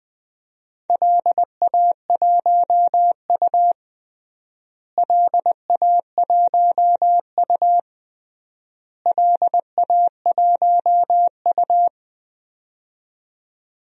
Morse code, heard as a keyed tone, about 20 wpm, 700 Hz. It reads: LA1U LA1U LA1U
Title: DXCC-100 Call Signs - 4 Characters - Repeated 3x - 20wpm